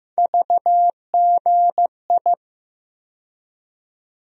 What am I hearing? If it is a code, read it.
Morse code, heard as a keyed tone, about 15 wpm, 700 Hz. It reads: VGI
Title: USA Call Sign Postfixes - 3 Letters 15wpm